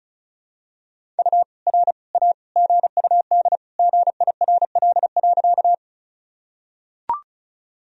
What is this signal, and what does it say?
Morse code, heard as a keyed tone, about 36 wpm, 700 Hz. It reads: U R A GUD GIRL.
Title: Sentences from Top 400 Words and Abbreviations 35wpm